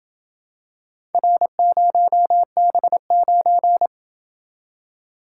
Morse code, heard as a keyed tone, about 28 wpm, 700 Hz. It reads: R0B9